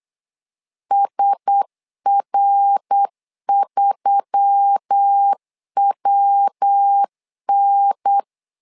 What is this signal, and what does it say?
Morse code, heard as a keyed tone, about 8 wpm, 800 Hz. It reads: SR3WN
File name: LEZIONI / 4 Q1S3 8wpm 3037